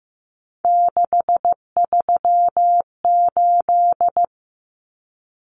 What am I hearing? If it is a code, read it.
Morse code, heard as a keyed tone, about 15 wpm, 700 Hz. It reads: 638
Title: Numbers - 3-Digit 15wpm